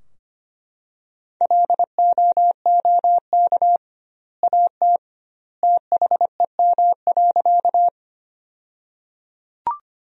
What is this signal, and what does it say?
Morse code, heard as a keyed tone, about 25 wpm, 700 Hz. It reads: LOOK AT THEM.
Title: 2x Word-Spacing - Sentences from Top 100 Words - 25wpm